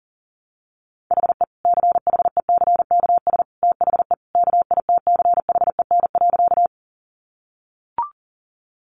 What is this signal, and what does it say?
Morse code, heard as a keyed tone, about 40 wpm, 700 Hz. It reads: HE CHECKS THE KITCHEN.
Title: Easy Phrases 40wpm